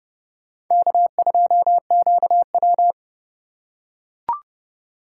Morse code, heard as a keyed tone, about 31 wpm, 700 Hz. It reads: K2QW